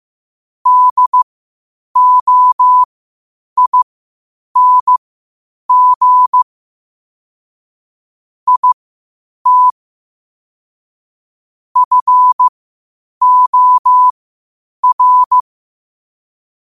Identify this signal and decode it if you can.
Morse code, heard as a keyed tone, about 15 wpm, spaced as for 8 wpm, 1000 Hz. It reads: DOING IT FOR
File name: qotdfiles / 8wpm